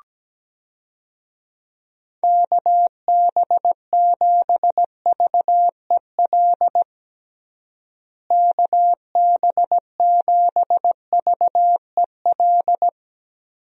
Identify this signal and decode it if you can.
Morse code, heard as a keyed tone, about 17 wpm, 700 Hz. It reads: KB7VEL KB7VEL
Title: USA Call Signs - 6 Characters - Repeated 3x - 17wpm